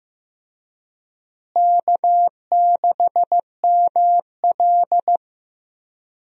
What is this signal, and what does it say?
Morse code, heard as a keyed tone, about 15 wpm, 700 Hz. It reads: K6ML
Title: USA Call Signs - 4 Characters - Repeated 2x - 15wpm